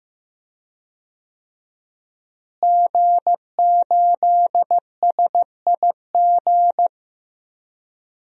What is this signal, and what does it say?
Morse code, heard as a keyed tone, about 15 wpm, 700 Hz. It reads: G8SIG
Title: DXCC-100 Call Signs - 5 Characters - Repeated 3x - 15wpm